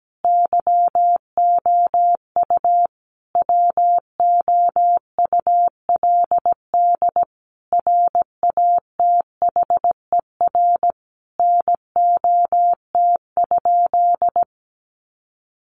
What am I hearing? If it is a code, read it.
Morse code, heard as a keyed tone, about 17 wpm, 700 Hz. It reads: YOU WOULD RATHER NOT?
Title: Sentences from Top 700 Words 17wpm